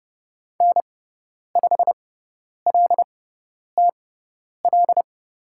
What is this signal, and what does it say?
Morse code, heard as a keyed tone, about 30 wpm, spaced as for 10 wpm, 700 Hz. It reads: N5LTL
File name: Lesson 17g - Letter u - Call Signs 30wpm